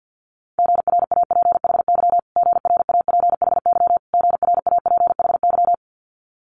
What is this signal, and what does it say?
Morse code, heard as a keyed tone, about 50 wpm, 700 Hz. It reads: GRAPHY GRAPHY GRAPHY
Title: Multi-letter Phonemes Repeated 3x 50wpm